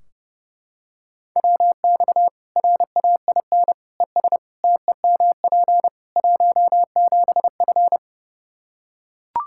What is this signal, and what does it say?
Morse code, heard as a keyed tone, about 32 wpm, 700 Hz. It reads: WX RAIN ES TEMP 17F
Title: Practice QSO - Exchange 2 - Info - 30wpm